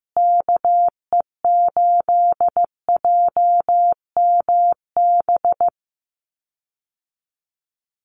Morse code, heard as a keyed tone, about 14 wpm, 700 Hz. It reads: KE8JMB
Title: USA Call Signs Repeated 3x 15wpm